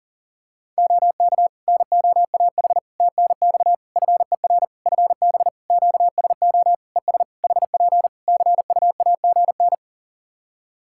Morse code, heard as a keyed tone, about 40 wpm, 700 Hz. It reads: OK NOAH TNX FER FB QSO ES HP CUAGN
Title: Practice QSO - Exchange 4 - Ending - 40wpm